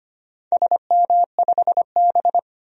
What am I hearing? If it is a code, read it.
Morse code, heard as a keyed tone, about 25 wpm, 700 Hz. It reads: SM5B
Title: DXCC-100 Call Signs - 4 Characters - 25wpm